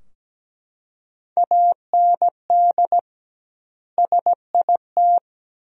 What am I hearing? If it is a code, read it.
Morse code, heard as a keyed tone, about 17 wpm, 700 Hz. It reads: AND SIT
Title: 2x Word-Spacing - Sentences from Top 500 Words - 17wpm